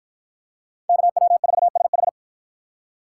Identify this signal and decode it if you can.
Morse code, heard as a keyed tone, about 48 wpm, 700 Hz. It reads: KW4SH